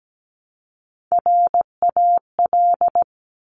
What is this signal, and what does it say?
Morse code, heard as a keyed tone, about 17 wpm, 700 Hz. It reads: RAL
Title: Top Three Letter Combinations in Words 17wpm